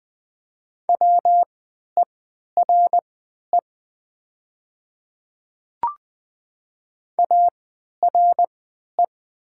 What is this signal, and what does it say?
Morse code, heard as a keyed tone, about 20 wpm, spaced as for 11 wpm, 700 Hz. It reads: WERE ARE
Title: 3x Character Spacing - Top 100 Words - 20wpm